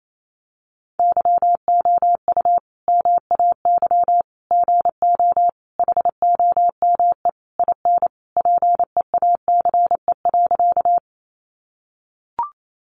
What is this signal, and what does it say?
Morse code, heard as a keyed tone, about 28 wpm, 700 Hz. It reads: YOU MAY GO HOME IN PEACE.
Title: Sentences from Top 1500 Words 28wpm